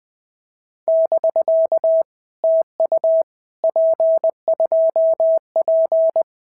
Morse code, heard as a keyed tone, about 20 wpm, 650 Hz. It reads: <BK> TU P2P
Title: POTA QSO Practice - 20wpm